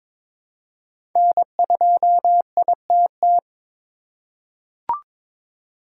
Morse code, heard as a keyed tone, about 21 wpm, 700 Hz. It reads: N2ITT
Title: USA Call Signs - 5 Characters - 3x Character Spacing - 22wpm